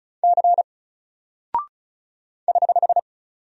Morse code, heard as a keyed tone, about 36 wpm, 700 Hz. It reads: C <HH>
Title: Single Character (Letter:Number:Punctuation:Prosign) Alt 35wpm